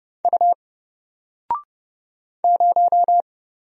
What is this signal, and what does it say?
Morse code, heard as a keyed tone, about 30 wpm, 700 Hz. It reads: U 0
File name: Single-Letter-Number-30wpm